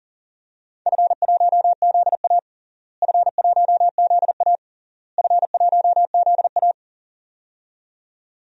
Morse code, heard as a keyed tone, about 42 wpm, 700 Hz. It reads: F1ZA F1ZA F1ZA
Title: DXCC-100 Call Signs - 4 Characters - Repeated 3x - 40wpm